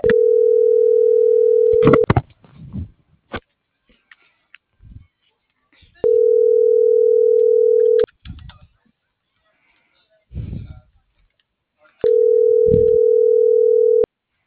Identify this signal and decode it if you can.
Ringback tone (North America), 440 + 480 Hz.
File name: CENTRAL MULTIMODAL / out viridiana.juarez